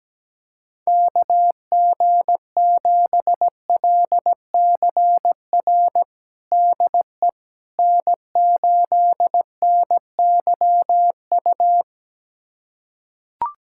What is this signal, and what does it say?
Morse code, heard as a keyed tone, about 17 wpm, 700 Hz. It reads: KG7LCR DE N8NYU
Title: USA Callsign DE Callsign 17wpm